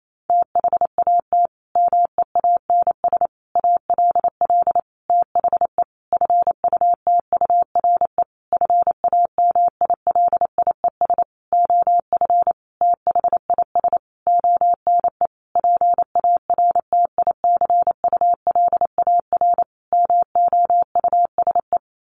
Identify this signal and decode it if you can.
Morse code, heard as a keyed tone, about 28 wpm, 700 Hz. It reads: THAT MEANS ALL THE FUTURE FAMILIES OF THIS ONE PARTICULAR MOUSE
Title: A Sound Of Thunder 28wpm